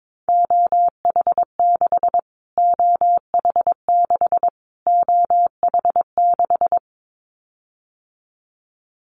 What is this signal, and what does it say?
Morse code, heard as a keyed tone, about 22 wpm, 700 Hz. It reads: OH6 OH6 OH6
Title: Callsign Prefixes for Top 25 DXCC - Repeated 3x - 22wpm